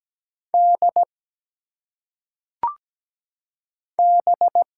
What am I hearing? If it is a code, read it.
Morse code, heard as a keyed tone, about 17 wpm, 700 Hz. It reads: D B
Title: Single Letters - Phonetic Pronunciation 17wpm